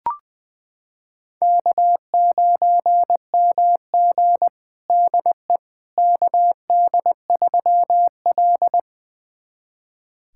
Morse code, heard as a keyed tone, about 20 wpm, 700 Hz. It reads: K9MG DE KD3L
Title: USA Callsign DE Callsign - 4 Characters - 20wpm